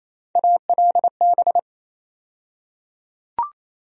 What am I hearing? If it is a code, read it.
Morse code, heard as a keyed tone, about 28 wpm, 700 Hz. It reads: ALB
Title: USA Call Sign Postfixes - 3 Letters 28wpm